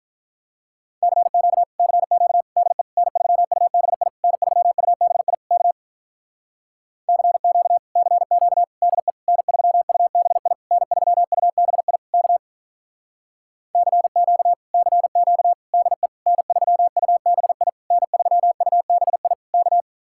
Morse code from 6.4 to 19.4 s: CQ CQ DE N3UBI N3UBI K CQ CQ DE N3UBI N3UBI

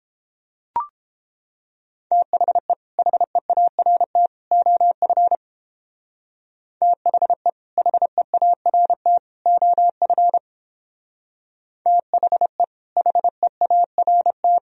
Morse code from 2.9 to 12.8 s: HEART OF THE HEART OF THE

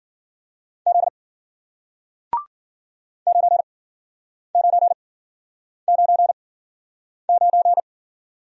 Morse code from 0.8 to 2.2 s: Z

